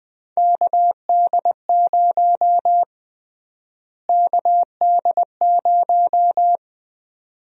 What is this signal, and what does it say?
Morse code, heard as a keyed tone, about 20 wpm, 700 Hz. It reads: KD0 KD0
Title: USA Call Sign Prefixes - Repeated 2x - 20wpm